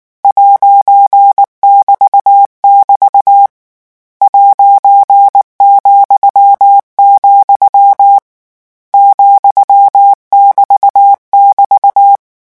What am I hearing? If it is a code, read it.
Morse code, heard as a keyed tone, about 19 wpm, 800 Hz. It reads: '<BT><BT> ',, ,<BT><BT>
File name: morse mp3 / lecon14-vitesse 18